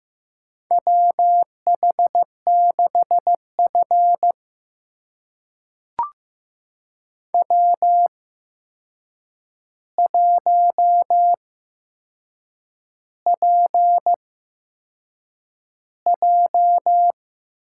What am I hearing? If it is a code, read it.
Morse code, heard as a keyed tone, about 15 wpm, spaced as for 6 wpm, 700 Hz. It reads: WH6F W1PJ
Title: USA Call Signs - 4 Characters - 8x Character Spacing - 15wpm